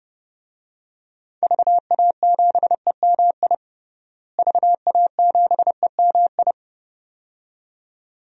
Morse code, heard as a keyed tone, about 32 wpm, 700 Hz. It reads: VA7EMI VA7EMI
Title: DXCC-100 Call Signs - 6 Characters - Repeated 2x - 30wpm